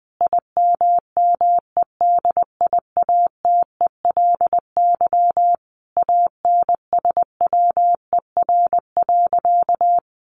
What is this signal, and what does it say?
Morse code, heard as a keyed tone, about 20 wpm, 700 Hz. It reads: IMMEDIATELY ANSWER.
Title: Sentences from Top 1100 Words 20wpm